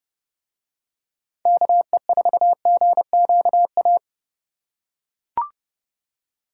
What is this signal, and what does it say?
Morse code, heard as a keyed tone, about 29 wpm, 700 Hz. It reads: KE4GQA